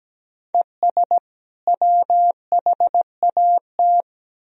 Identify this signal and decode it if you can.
Morse code, heard as a keyed tone, about 17 wpm, 700 Hz. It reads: ES WHAT
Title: Sentences from Top 100 Words and Abbreviations 17wpm